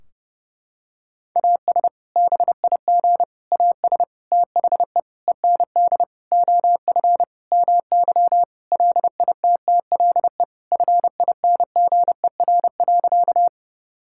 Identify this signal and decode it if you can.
Morse code, heard as a keyed tone, about 30 wpm, 700 Hz. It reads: AS BIG AS THE END OF MY LITTLE FINGER.
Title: Sentences from Top 1600 Words 30wpm